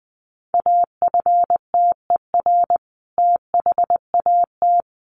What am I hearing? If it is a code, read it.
Morse code, heard as a keyed tone, about 20 wpm, 700 Hz. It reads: AFTER THAT
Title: Sentences from Top 1100 Words 20wpm